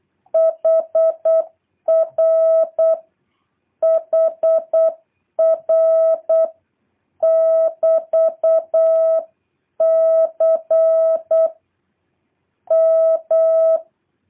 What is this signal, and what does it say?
Morse code, heard as a keyed tone, about 8 wpm, 650 Hz. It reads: HR HR <BT>C M
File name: jh8cbh / 2023cw